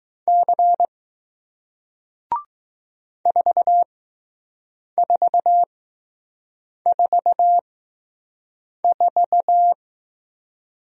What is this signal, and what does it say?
Morse code, heard as a keyed tone, about 20 wpm, spaced as for 6 wpm, 700 Hz. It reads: C 4444